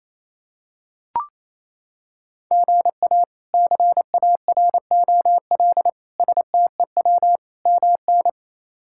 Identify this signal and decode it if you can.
Morse code, heard as a keyed tone, about 28 wpm, 700 Hz. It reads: GA CAROL STEW MN
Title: K1USN SST QSO Practice - 28wpm